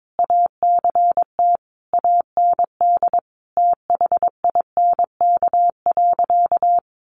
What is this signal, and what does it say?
Morse code, heard as a keyed tone, about 22 wpm, 700 Hz. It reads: ACT AND THINK.
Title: Sentences from Top 500 Words 22wpm